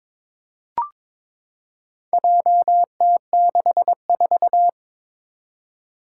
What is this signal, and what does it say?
Morse code, heard as a keyed tone, about 22 wpm, 700 Hz. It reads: JT64
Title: Four Challenging Letter-Number Combinations 22wpm